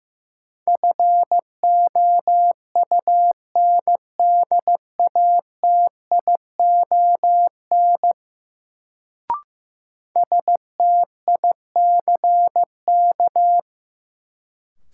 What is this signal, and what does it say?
Morse code, heard as a keyed tone, about 15 wpm, 700 Hz. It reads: FOUNDATION STICK
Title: Sentences from Top 1200 Words - New Word Drill Alt - 15wpm